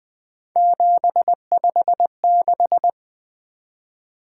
Morse code, heard as a keyed tone, about 20 wpm, 700 Hz. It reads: 756
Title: Three Letter-Number Combinations - 20wpm